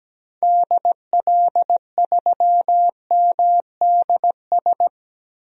Morse code, heard as a keyed tone, about 17 wpm, 700 Hz. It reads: DL3MDS
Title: DXCC-100 Call Signs - 6 Characters - 17wpm